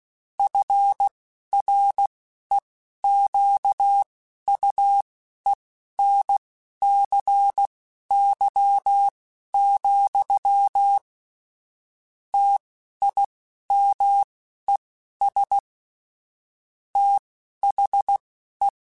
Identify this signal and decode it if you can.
Morse code, heard as a keyed tone, about 16 wpm, spaced as for 11 wpm, 800 Hz. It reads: FREQUENCY, TIMES THE
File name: Technical Subjects / Technical Subjects-mp3s / XC @10